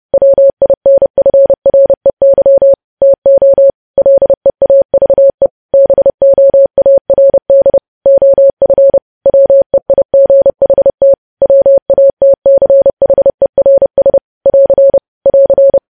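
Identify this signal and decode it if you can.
Morse code, heard as a keyed tone, about 30 wpm, 550 Hz. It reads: WINFREY TO LEAVE BOARD OF WEIGHT WATCHERS <AR> <AR>
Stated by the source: Morse news headlines sent at 30 wpm.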